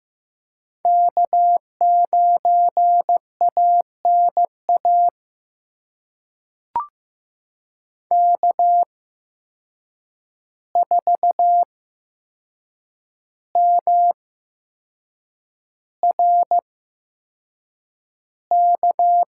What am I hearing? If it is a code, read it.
Morse code, heard as a keyed tone, about 15 wpm, spaced as for 6 wpm, 700 Hz. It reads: K9ANA K4MRK
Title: USA Call Signs - 5 Characters - 8x Character Spacing - 15wpm